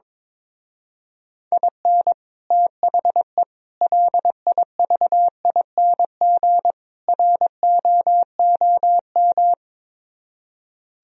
Morse code, heard as a keyed tone, about 22 wpm, 700 Hz. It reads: IN THE LIVING ROOM
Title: Speed-Racing - Sets of 4 Words 22wpm